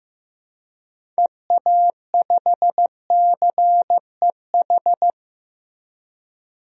Morse code, heard as a keyed tone, about 15 wpm, 700 Hz. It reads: EA5CEH
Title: DXCC-100 Call Signs - 6 Characters - Repeated 2x - 15wpm